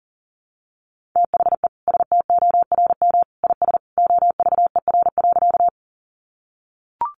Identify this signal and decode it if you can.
Morse code, heard as a keyed tone, about 40 wpm, 700 Hz. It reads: THE STORM IS OVER.
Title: Sentences from Top 1500 Words 40wpm